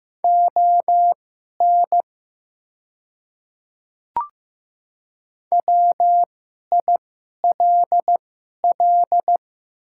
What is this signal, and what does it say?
Morse code, heard as a keyed tone, about 15 wpm, spaced as for 11 wpm, 700 Hz. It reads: ON WILL
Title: 2x Character Spacing - Top 200 Words - 15wpm